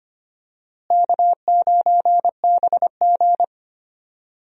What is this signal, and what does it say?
Morse code, heard as a keyed tone, about 25 wpm, 700 Hz. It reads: K9BG
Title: USA Call Signs - 4 Characters - Repeated 2x - 25wpm